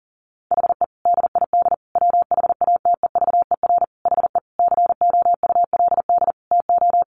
Morse code, heard as a keyed tone, about 40 wpm, 700 Hz. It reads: HE DID WHATEVER HE COULD TO
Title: Sentences from Top 1100 Words 40wpm